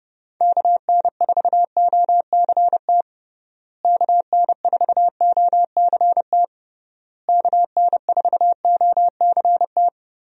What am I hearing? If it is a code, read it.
Morse code, heard as a keyed tone, about 30 wpm, 700 Hz. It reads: KN4OCT KN4OCT KN4OCT
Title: USA Call Signs - 6 Characters - Repeated 3x - 30wpm